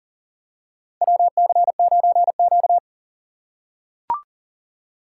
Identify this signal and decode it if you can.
Morse code, heard as a keyed tone, about 40 wpm, 700 Hz. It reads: WC9Q
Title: USA Call Signs - 4 Characters - 4x Character Spacing - 40wpm